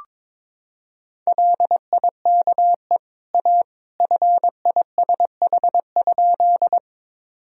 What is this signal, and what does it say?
Morse code, heard as a keyed tone, about 22 wpm, 700 Hz. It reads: LIKE A FISH?
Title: Sentences from Top 1000 Words - Review Part 1 - 22wpm